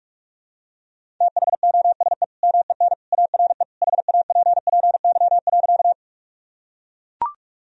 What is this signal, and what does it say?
Morse code, heard as a keyed tone, about 43 wpm, 700 Hz. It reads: THOSE MEN ARE HAPPY.
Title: Sentences from Top 1000 Words - Review Part 1 - 45wpm